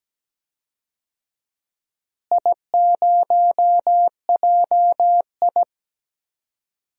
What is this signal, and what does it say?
Morse code, heard as a keyed tone, about 17 wpm, 700 Hz. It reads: I0JI
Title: DXCC-100 Call Signs - 4 Characters - Repeated 3x - 17wpm